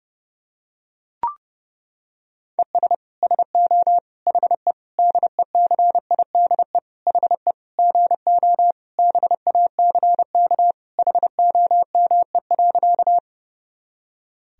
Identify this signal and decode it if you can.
Morse code, heard as a keyed tone, about 30 wpm, 700 Hz. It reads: ES SO HE DECIDE HE GO BACK HOME.
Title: Sentences from Top 500 Words and Abbreviations 30wpm